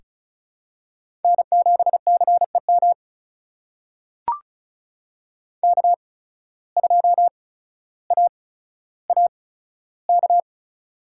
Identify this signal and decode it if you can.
Morse code, heard as a keyed tone, about 36 wpm, spaced as for 15 wpm, 700 Hz. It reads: N7CEM K2AAK
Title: USA Call Signs - 5 Characters - 8x Character Spacing - 35wpm